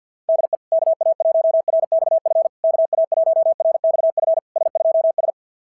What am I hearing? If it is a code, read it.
Morse code, heard as a keyed tone, about 50 wpm, 660 Hz. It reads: DE KA1RXF KA1RXF S2S
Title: SOTA QSO Practice 50wpm